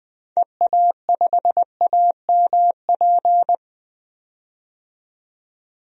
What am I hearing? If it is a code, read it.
Morse code, heard as a keyed tone, about 20 wpm, 700 Hz. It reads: EA5AMP
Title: DXCC-100 Call Signs - 6 Characters - 20wpm